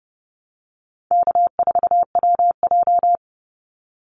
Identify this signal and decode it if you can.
Morse code, heard as a keyed tone, about 30 wpm, 700 Hz. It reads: K4WJ
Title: Call Signs Repeated 2x - 30wpm